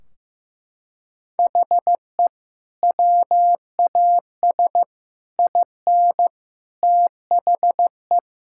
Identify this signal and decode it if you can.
Morse code, heard as a keyed tone, about 15 wpm, 700 Hz. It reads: HE WAS IN THE
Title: Sentences from Top 1600 Words 15wpm